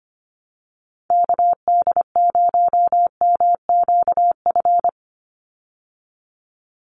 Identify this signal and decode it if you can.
Morse code, heard as a keyed tone, about 25 wpm, 700 Hz. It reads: KD0MQF
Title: USA Call Signs 25wpm